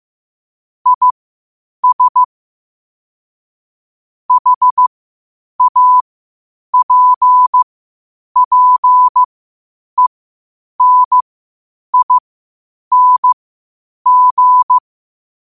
Morse code, heard as a keyed tone, about 15 wpm, spaced as for 8 wpm, 1000 Hz. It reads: IS HAPPENING
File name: 8wpm